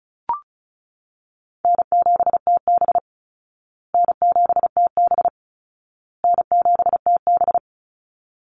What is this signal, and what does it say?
Morse code, heard as a keyed tone, about 34 wpm, 700 Hz. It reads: N7TB N7TB N7TB
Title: Call Signs Repeated 2x - 35wpm